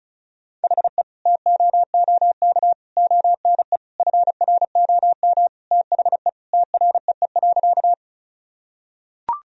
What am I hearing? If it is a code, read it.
Morse code, heard as a keyed tone, about 36 wpm, 700 Hz. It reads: HE TOOK ONE FROM THE TREE.
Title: Sentences from Top 1000 Words - Review Part 1 - 35wpm